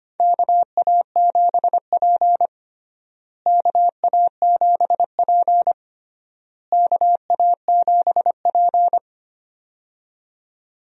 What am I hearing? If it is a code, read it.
Morse code, heard as a keyed tone, about 25 wpm, 700 Hz. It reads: KA7P KA7P KA7P
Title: USA Call Signs - 4 Characters - Repeated 3x - 25wpm